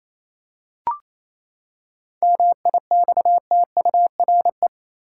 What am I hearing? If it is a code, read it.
Morse code, heard as a keyed tone, about 28 wpm, 700 Hz. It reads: MIXTURE
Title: Seven Letter Words - 28wpm